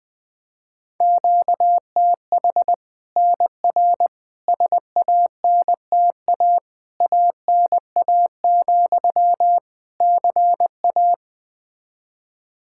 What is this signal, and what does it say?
Morse code, heard as a keyed tone, about 20 wpm, 700 Hz. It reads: QTH NR SANTA ANA, CA
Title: Practice QSO - Exchange 1 - Beginning - 20wpm